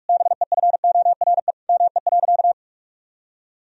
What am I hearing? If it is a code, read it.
Morse code, heard as a keyed tone, about 41 wpm, 700 Hz. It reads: BEFORE ME.
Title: Sentences from Top 1000 Words - Review Part 1 - 45wpm